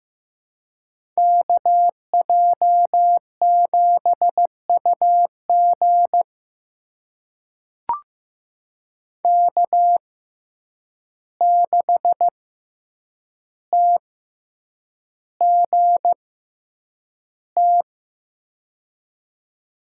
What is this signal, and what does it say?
Morse code, heard as a keyed tone, about 16 wpm, spaced as for 8 wpm, 700 Hz. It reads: KJ7UG K6TGT
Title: USA Call Signs - 5 Characters - 6x Character Spacing - 15wpm